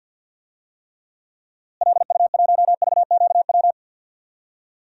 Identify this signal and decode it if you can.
Morse code, heard as a keyed tone, about 48 wpm, 700 Hz. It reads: LU1VQW
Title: DXCC-100 Call Signs - 6 Characters - Repeated 2x - 50wpm